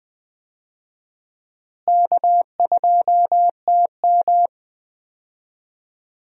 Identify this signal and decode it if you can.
Morse code, heard as a keyed tone, about 21 wpm, 700 Hz. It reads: K2TM
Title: USA Call Signs - 4 Characters - Repeated 2x - 20wpm